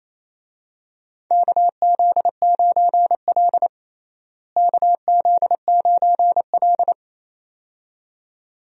Morse code, heard as a keyed tone, about 27 wpm, 700 Hz. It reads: KZ9L KZ9L
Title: USA Call Signs - 4 Characters - Repeated 2x - 28wpm